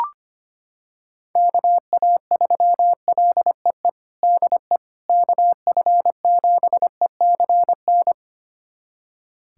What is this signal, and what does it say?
Morse code, heard as a keyed tone, about 25 wpm, 700 Hz. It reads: KA3LEE DE KF7ECN
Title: USA Callsign DE Callsign - 6 Characters - 25wpm